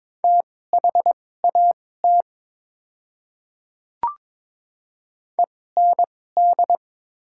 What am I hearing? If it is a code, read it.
Morse code, heard as a keyed tone, about 22 wpm, spaced as for 16 wpm, 700 Hz. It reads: THAT END
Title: 2x Character Spacing - Top 200 Words - 22wpm